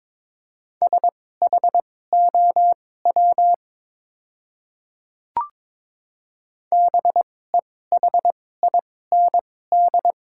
Morse code, heard as a keyed tone, about 22 wpm, spaced as for 16 wpm, 700 Hz. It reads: SHOW BEHIND
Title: 2x Character Spacing - Top 500 Words - 22wpm